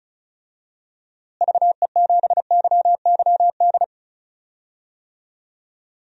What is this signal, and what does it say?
Morse code, heard as a keyed tone, about 36 wpm, 700 Hz. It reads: VE7YYD
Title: DXCC-100 Call Signs - 6 Characters - Repeated 2x - 35wpm